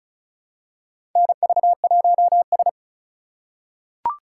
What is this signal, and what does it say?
Morse code, heard as a keyed tone, about 36 wpm, 700 Hz. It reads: NV1S
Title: USA Call Signs - 4 Characters - 3x Character Spacing - 35wpm